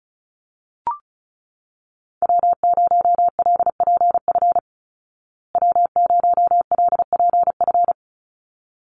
Morse code, heard as a keyed tone, about 34 wpm, 700 Hz. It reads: W0LPF W0LPF
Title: Call Signs Repeated 3x - 35wpm